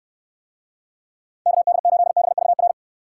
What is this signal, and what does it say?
Morse code, heard as a keyed tone, about 69 wpm, 700 Hz. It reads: KD8BVR